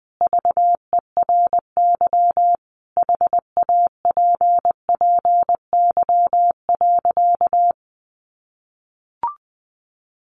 Morse code, heard as a keyed tone, about 20 wpm, 700 Hz. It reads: VERY HAPPY.
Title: Sentences from Top 1000 Words 20wpm